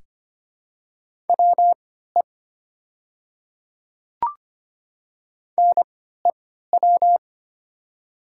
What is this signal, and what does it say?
Morse code, heard as a keyed tone, about 25 wpm, spaced as for 14 wpm, 690 Hz. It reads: WE NEW
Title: 3x Character Spacing - Top 200 Words - 25wpm